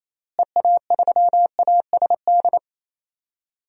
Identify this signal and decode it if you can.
Morse code, heard as a keyed tone, about 28 wpm, 700 Hz. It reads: EA3ASD
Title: DXCC-100 Call Signs - 6 Characters - Repeated 3x - 28wpm